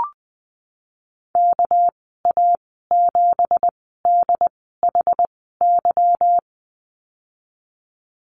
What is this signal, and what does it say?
Morse code, heard as a keyed tone, about 20 wpm, spaced as for 15 wpm, 700 Hz. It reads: KA7DHY
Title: 219 Lesson 35g - Number 0 - Call Signs 20wpm